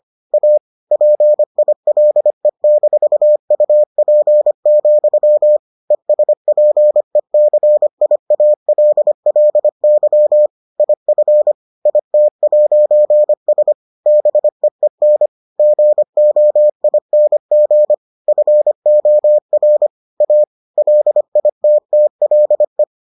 Morse code, heard as a keyed tone, about 25 wpm, 600 Hz. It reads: A PILE-UP, ESPECIALLY IF IT'S BEEN GOING FOR A LITTLE